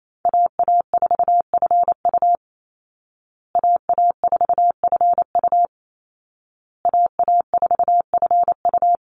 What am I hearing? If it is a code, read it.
Morse code, heard as a keyed tone, about 28 wpm, 700 Hz. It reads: AA4FU AA4FU AA4FU
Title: Call Signs Repeated 2x - 28wpm